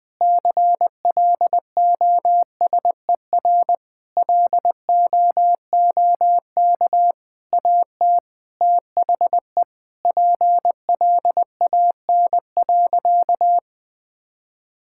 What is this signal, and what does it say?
Morse code, heard as a keyed tone, about 20 wpm, 700 Hz. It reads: CLOSER LOOK AT THE PLAN.
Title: Sentences with 6-Word Phrases - Spoken Beginning - 20wpm